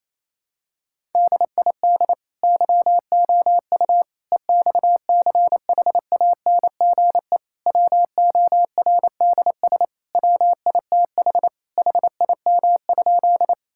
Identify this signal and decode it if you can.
Morse code, heard as a keyed tone, about 27 wpm, 700 Hz. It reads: DID YOU EXCHANGE WORDS WITH HIM?
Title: Sentences from Top 1000 Words - Review Part 6 - 28wpm